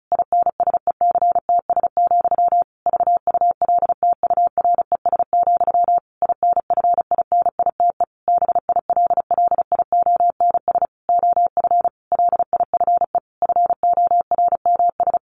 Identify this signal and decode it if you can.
Morse code, heard as a keyed tone, about 34 wpm, 700 Hz. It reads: INSECTS, VULTURES, INFINITE BILLIONS OF LIFE FORMS
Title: A Sound Of Thunder 35wpm